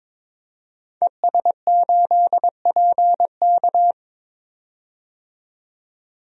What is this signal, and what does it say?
Morse code, heard as a keyed tone, about 22 wpm, 700 Hz. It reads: ES8PK
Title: DXCC-100 Call Signs - 5 Characters - 22wpm